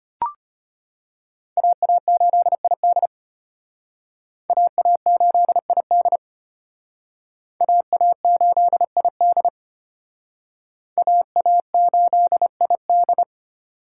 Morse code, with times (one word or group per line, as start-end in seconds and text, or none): 1.6-3.1 AA8ID
4.5-6.2 AA8ID
7.6-9.5 AA8ID
11.0-13.2 AA8ID